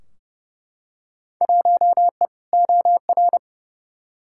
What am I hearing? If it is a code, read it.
Morse code, heard as a keyed tone, about 30 wpm, 700 Hz. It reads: 1E OR
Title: Field Day Exchanges - 30wpm